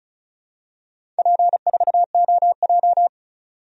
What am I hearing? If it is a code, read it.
Morse code, heard as a keyed tone, about 36 wpm, 700 Hz. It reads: P4OJ
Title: DXCC-100 Call Signs - 4 Characters - 35wpm